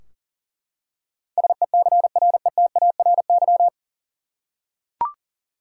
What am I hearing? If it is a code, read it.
Morse code, heard as a keyed tone, about 40 wpm, 700 Hz. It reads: SECRETARY
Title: Practice QSO - Exchange 3 - More Info - 40wpm